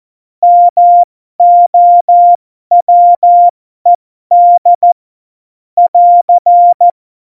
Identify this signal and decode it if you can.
Morse code, heard as a keyed tone, about 14 wpm, spaced as for 12 wpm, 700 Hz. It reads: MOWED <AR>
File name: Morsekurs / Witze